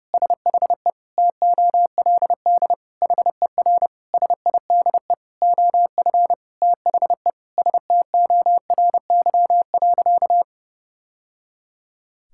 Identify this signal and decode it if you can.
Morse code, heard as a keyed tone, about 32 wpm, 700 Hz. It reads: SHE TOLD HER SIDE OF THE STORY.
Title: Sentences from Top 1000 Words - Review Part 6 - 30wpm